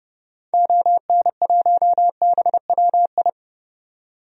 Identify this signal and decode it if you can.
Morse code, heard as a keyed tone, about 32 wpm, 700 Hz. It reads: ON1BWI
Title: DXCC-100 Call Signs - 6 Characters - 30wpm